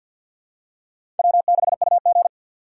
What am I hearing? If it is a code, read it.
Morse code, heard as a keyed tone, about 52 wpm, 700 Hz. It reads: W6UG
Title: USA Call Signs - 4 Characters - 6x Character Spacing - 50wpm